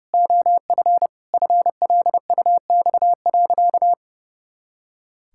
Morse code, heard as a keyed tone, about 30 wpm, 700 Hz. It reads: OF FLUX.
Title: Pangram Challenge v2 - 30wpm